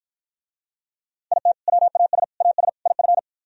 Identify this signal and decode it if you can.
Morse code, heard as a keyed tone, about 56 wpm, 700 Hz. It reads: IT WAS AS IF